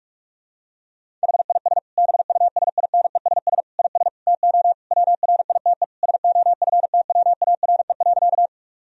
Morse code, heard as a keyed tone, about 44 wpm, 700 Hz. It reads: HIS BUSINESS IS TO WRITE SOFTWARE.
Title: Sentences from Top 1000 Words - Review Part 3 - 45wpm